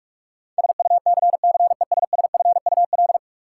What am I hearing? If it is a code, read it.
Morse code, heard as a keyed tone, about 42 wpm, 700 Hz. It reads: SUCCESSFUL
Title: Sentences from Top 900 Words - New Word Drill - 45wpm